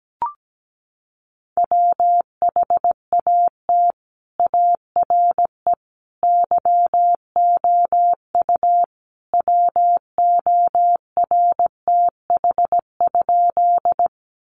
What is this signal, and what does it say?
Morse code, heard as a keyed tone, about 17 wpm, 700 Hz. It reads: WHAT ARE YOU WORTH?
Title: Sentences from Top 800 Words 17wpm